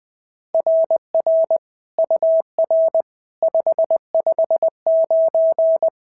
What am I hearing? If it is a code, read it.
Morse code, heard as a keyed tone, about 20 wpm, 650 Hz. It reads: RR UR 559
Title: SOTA QSO Practice 20wpm